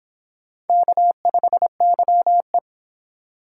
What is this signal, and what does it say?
Morse code, heard as a keyed tone, about 23 wpm, 700 Hz. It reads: K5YE